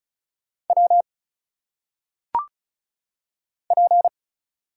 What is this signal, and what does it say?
Morse code, heard as a keyed tone, about 36 wpm, 700 Hz. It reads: W P
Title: Single Letters - Phonetic Pronunciation 35wpm